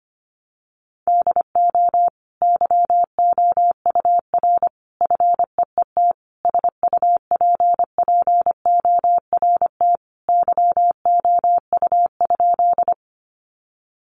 Morse code, heard as a keyed tone, about 25 wpm, 700 Hz. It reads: DO YOUR FEET SUPPORT YOU?
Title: Sentences from Top 800 Words 25wpm